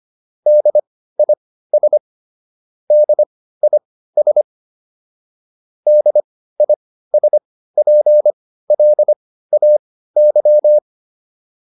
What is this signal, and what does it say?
Morse code, heard as a keyed tone, about 25 wpm, spaced as for 15 wpm, 600 Hz. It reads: DIS DIS DISPLAY